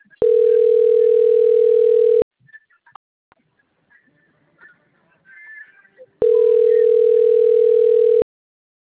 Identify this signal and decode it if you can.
Ringback tone (North America), 440 + 480 Hz.